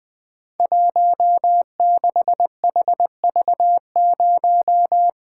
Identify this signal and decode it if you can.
Morse code, heard as a keyed tone, about 21 wpm, 700 Hz. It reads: 16HV0
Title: Code Groups - 5 Letter:Numbers - 20wpm